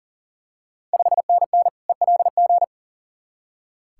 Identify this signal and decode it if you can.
Morse code, heard as a keyed tone, about 37 wpm, 690 Hz. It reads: 5NN ELG
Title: Ontario QSO Party Exchange 40wpm